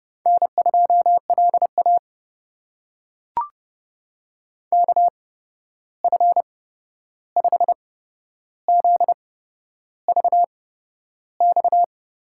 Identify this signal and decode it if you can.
Morse code, heard as a keyed tone, about 30 wpm, spaced as for 11 wpm, 700 Hz. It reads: N2LA KF5ZVX
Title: USA Call Signs - 8x Character Spacing - 30wpm